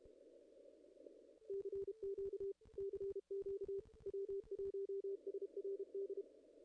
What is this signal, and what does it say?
Morse code, heard as a keyed tone, about 32 wpm, 440 Hz. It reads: CQ CQ W1SRD
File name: contests / ss-cw-k5tr-op-n5rz -0000